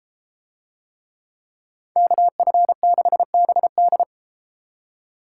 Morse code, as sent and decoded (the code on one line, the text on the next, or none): -.- ..-. -.... -... -..
KF6BD